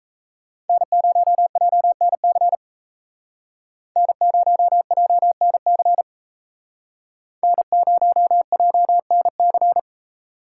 Morse code, heard as a keyed tone, about 36 wpm, 700 Hz. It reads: N0JNC N0JNC N0JNC